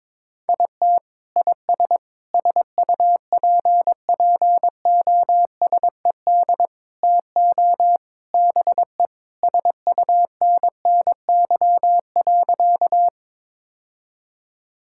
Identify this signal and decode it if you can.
Morse code, heard as a keyed tone, about 22 wpm, 700 Hz. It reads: IT IS SUPPOSED TO BE SUNNY.
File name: practice / 22wpm / Sentences with 3-Word Phrases - 22wpm